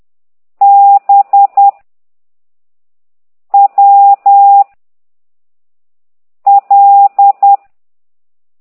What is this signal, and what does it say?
Morse code, heard as a keyed tone, about 10 wpm, 800 Hz. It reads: B W L
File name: ct2idw / exerc15